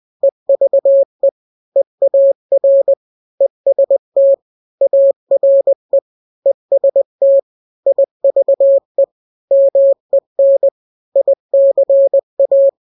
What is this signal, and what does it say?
Morse code, heard as a keyed tone, about 20 wpm, 550 Hz. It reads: EVE EAR EST ARE EST IVE MEN ICA
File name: n-grams / 3-grams 20 18 WPM